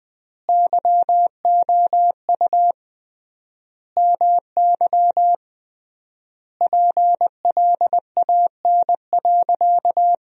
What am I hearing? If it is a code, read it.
Morse code, heard as a keyed tone, about 21 wpm, 700 Hz. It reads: YOU MY PLAN.
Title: 3x Word-Spacing - Sentences from Top 500 Words - 20wpm